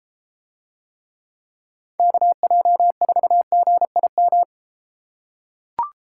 Morse code, heard as a keyed tone, about 32 wpm, 700 Hz. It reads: KJ4GIM